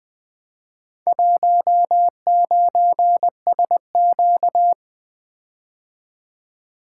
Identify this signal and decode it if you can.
Morse code, heard as a keyed tone, about 20 wpm, 700 Hz. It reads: 19SQ
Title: Four Letter-Number Combinations - 20wpm